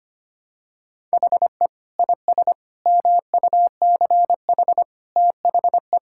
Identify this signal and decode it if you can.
Morse code, heard as a keyed tone, about 25 wpm, 700 Hz. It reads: HE IS MUCH THE
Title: Sentences from Top 1600 Words 25wpm